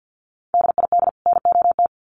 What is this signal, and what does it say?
Morse code, heard as a keyed tone, about 50 wpm, 700 Hz. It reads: DID NOT
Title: Sets of 2 Words 50wpm